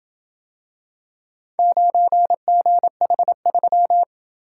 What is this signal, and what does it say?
Morse code, heard as a keyed tone, about 28 wpm, 700 Hz. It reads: 9GH3